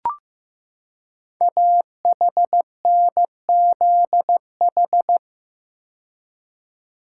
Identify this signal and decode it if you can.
Morse code, heard as a keyed tone, about 15 wpm, 700 Hz. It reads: AHNZH
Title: Code Groups - 5 Letters - 15wpm